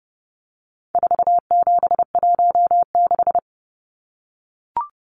Morse code, heard as a keyed tone, about 30 wpm, 700 Hz. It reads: 4716
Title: Numbers - 4-Digit 30wpm